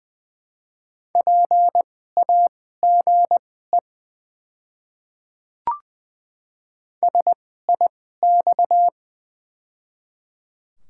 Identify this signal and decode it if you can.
Morse code, heard as a keyed tone, about 21 wpm, spaced as for 15 wpm, 690 Hz. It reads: PAGE SIX
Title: 2x Character Spacing - Top 500 Words - 20wpm